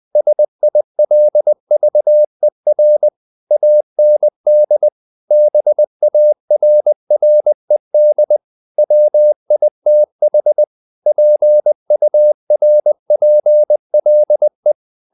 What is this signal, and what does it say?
Morse code, heard as a keyed tone, about 20 wpm, 600 Hz. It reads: SILVER AND BARRED WITH PURPLE
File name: the old man and the sea part 2 20wpm